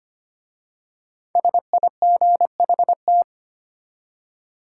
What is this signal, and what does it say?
Morse code, heard as a keyed tone, about 25 wpm, 700 Hz. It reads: SIGHT
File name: 5 Letter Words - First 3 Letters Given - 25wpm